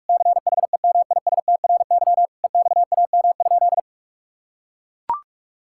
Morse code, heard as a keyed tone, about 42 wpm, 700 Hz. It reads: CHEMISTRY EXAM?
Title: Sentences from Top 1000 Words - Review Part 6 - 45wpm